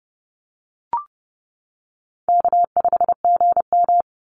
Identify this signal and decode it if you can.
Morse code, heard as a keyed tone, about 30 wpm, 700 Hz. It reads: K5GM
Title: Call Signs - 30wpm